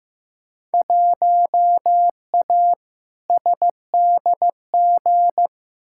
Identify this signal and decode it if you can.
Morse code, heard as a keyed tone, about 16 wpm, 700 Hz. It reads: 1A SDG
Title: Field Day Exchanges Encore - 15wpm